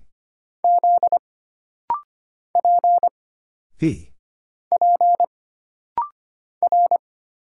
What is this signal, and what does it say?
Morse code, heard as a keyed tone, about 25 wpm, 700 Hz. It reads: Z P P R
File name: Single Character (Letter:Number:Punctuation:Prosign) - Rapid-Fire - 25wpm